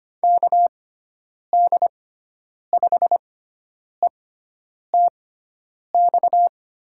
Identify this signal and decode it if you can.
Morse code, heard as a keyed tone, about 25 wpm, spaced as for 9 wpm, 700 Hz. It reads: KD5ETX
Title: USA Call Signs - 6 Characters - 6x Character Spacing - 25wpm